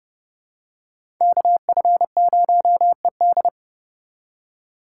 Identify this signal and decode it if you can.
Morse code, heard as a keyed tone, about 27 wpm, 700 Hz. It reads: KF0ED